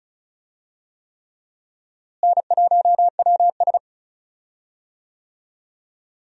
Morse code, heard as a keyed tone, about 34 wpm, 700 Hz. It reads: N1WS
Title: USA Call Signs - 4 Characters - Repeated 3x - 35wpm